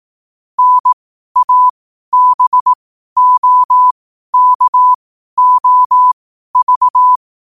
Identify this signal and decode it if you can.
Morse code, heard as a keyed tone, about 18 wpm, spaced as for 12 wpm, 1000 Hz. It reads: NABOKOV
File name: qotdfiles / 12wpm